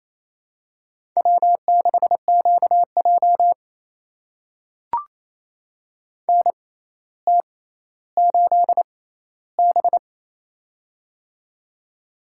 Morse code, heard as a keyed tone, about 28 wpm, spaced as for 14 wpm, 700 Hz. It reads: W6QJ NT8B
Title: USA Call Signs - 4 Characters - 6x Character Spacing - 28wpm